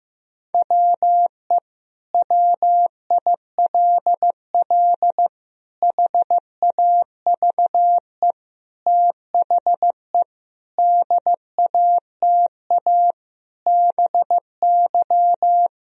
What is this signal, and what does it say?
Morse code, heard as a keyed tone, about 15 wpm, 700 Hz. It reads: WE WILL HAVE THE DATA BY
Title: Sentences from Top 1000 Words - Review Part 3 - 15wpm